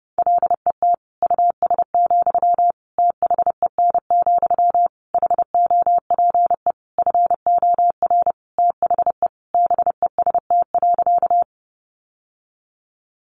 Morse code, heard as a keyed tone, about 30 wpm, 700 Hz. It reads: LET US, THEN, HOPE FOR THE BEST.
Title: Sentences from Top 1000 Words 30wpm